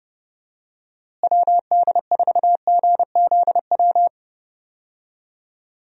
Morse code, heard as a keyed tone, about 30 wpm, 700 Hz. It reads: WD4GZW
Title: USA Call Signs - 6 Characters - Repeated 3x - 30wpm